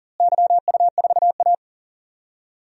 Morse code, heard as a keyed tone, about 37 wpm, 700 Hz. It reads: YU4A
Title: DXCC-100 Call Signs - 4 Characters - Repeated 3x - 40wpm